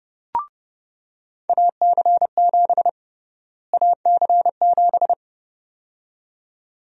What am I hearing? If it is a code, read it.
Morse code, heard as a keyed tone, about 30 wpm, 700 Hz. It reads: AC7 AC7
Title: USA Call Sign Prefixes - Repeated 2x - 30wpm